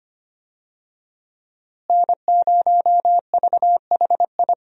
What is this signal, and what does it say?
Morse code, heard as a keyed tone, about 25 wpm, 700 Hz. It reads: N0VHI
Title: USA Call Signs - 5 Characters - Repeated 2x - 25wpm